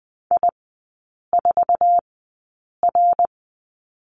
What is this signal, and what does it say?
Morse code, heard as a keyed tone, about 20 wpm, spaced as for 8 wpm, 700 Hz. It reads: I4R
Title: 074 Lesson 13e - Letter L - Three Characters 20wpm